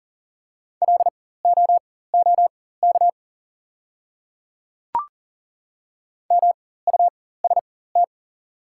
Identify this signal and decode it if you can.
Morse code, heard as a keyed tone, about 40 wpm, spaced as for 19 wpm, 690 Hz. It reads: LOOK MUST
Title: 4x Character Spacing - Top 300 Words - 40wpm